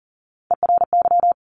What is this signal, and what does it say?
Morse code, heard as a keyed tone, about 38 wpm, 700 Hz. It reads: ERY
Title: Top Three Letter Combinations in Words 40wpm